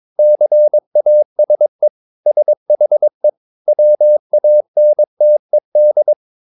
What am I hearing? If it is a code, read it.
Morse code, heard as a keyed tone, about 22 wpm, 600 Hz. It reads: CASE SHE WANTED
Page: Romanen Never af Ken Follet på engelsk.